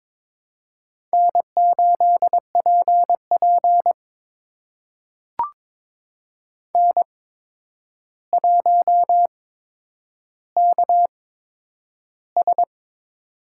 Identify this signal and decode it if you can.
Morse code, heard as a keyed tone, about 22 wpm, spaced as for 9 wpm, 700 Hz. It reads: N8PP N1KS
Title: USA Call Signs - 4 Characters - 8x Character Spacing - 22wpm